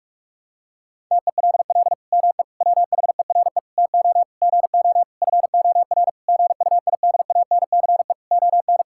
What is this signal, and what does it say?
Morse code, heard as a keyed tone, about 46 wpm, 700 Hz. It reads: TELL ME WHERE TO GO FOR GUIDANCE ON
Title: Sentences from Top 1600 Words 45wpm